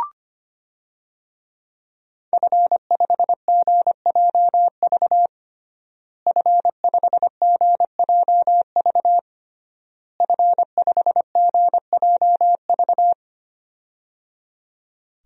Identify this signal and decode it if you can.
Morse code, heard as a keyed tone, about 25 wpm, 700 Hz. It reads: F5GJV F5GJV F5GJV
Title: DXCC-100 Call Signs - 5 Characters - Repeated 3x - 25wpm